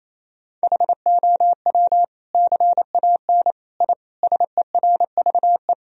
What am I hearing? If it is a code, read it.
Morse code, heard as a keyed tone, about 28 wpm, 700 Hz. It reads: HOW CAN I SERVE
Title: Sentences from Top 1000 Words - Review Part 2 - 28wpm